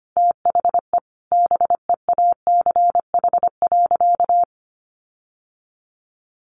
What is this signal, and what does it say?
Morse code, heard as a keyed tone, about 25 wpm, 700 Hz. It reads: THE BEACH.
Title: Sentences from Top 1500 Words 25wpm